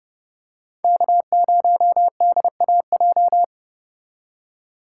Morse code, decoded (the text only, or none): K0DAJ